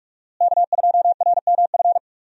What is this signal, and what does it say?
Morse code, heard as a keyed tone, about 42 wpm, 700 Hz. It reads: K2RMF
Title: USA Call Signs - 5 Characters - Repeated 2x - 45wpm